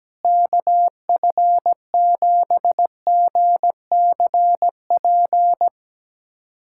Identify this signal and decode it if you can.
Morse code, heard as a keyed tone, about 17 wpm, 700 Hz. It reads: KF7GCP
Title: USA Callsign DE Callsign - 6 Characters - 17wpm